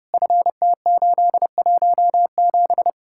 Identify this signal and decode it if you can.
Morse code, heard as a keyed tone, about 36 wpm, 700 Hz. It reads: FT817
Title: Speed-Racing Common QSO Phrases - 30wpm